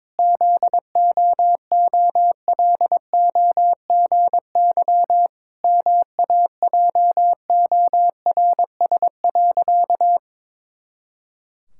Morse code, heard as a keyed tone, about 22 wpm, 700 Hz. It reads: ZOOLOGY MAJORS.
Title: Pangram Challenge v2 - 22wpm